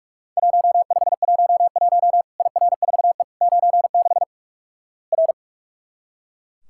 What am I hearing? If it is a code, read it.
Morse code, heard as a keyed tone, about 44 wpm, 700 Hz. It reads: 1511 IR4E 96 R
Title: Worked All Europe DX Contest - QTC Exchange - 45wpm